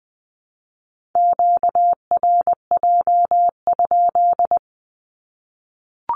Morse code, heard as a keyed tone, about 20 wpm, 700 Hz. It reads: QRJ?
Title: Q-Codes Repeated 3x 20wpm